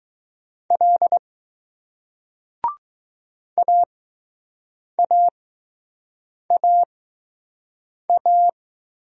Morse code, heard as a keyed tone, about 20 wpm, 700 Hz. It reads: L A A A A